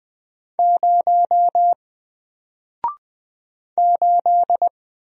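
Morse code, heard as a keyed tone, about 20 wpm, 700 Hz. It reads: 0 8
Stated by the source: Single letter and number Morse drill at 20 wpm, from N4LCD.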